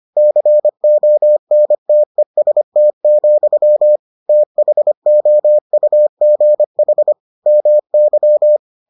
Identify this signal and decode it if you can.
Morse code, heard as a keyed tone, about 25 wpm, 600 Hz. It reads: CONTEST, THOUGH MY